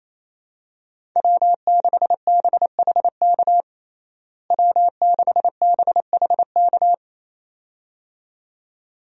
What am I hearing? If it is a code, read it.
Morse code, heard as a keyed tone, about 28 wpm, 700 Hz. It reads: W6BHK W6BHK
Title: USA Call Signs - 5 Characters - Repeated 2x - 28wpm